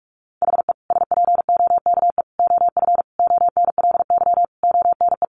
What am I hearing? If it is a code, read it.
Morse code, heard as a keyed tone, about 44 wpm, 700 Hz. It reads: HE SPOKE OF ONLY ONE
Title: Sentences from Top 1200 Words 45wpm